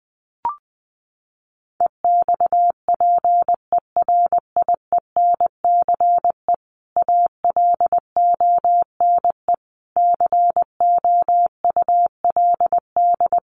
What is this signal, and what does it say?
Morse code, heard as a keyed tone, about 20 wpm, 700 Hz. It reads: EXPERIENCE ALONE COULD
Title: Sentences from Top 1000 Words 20wpm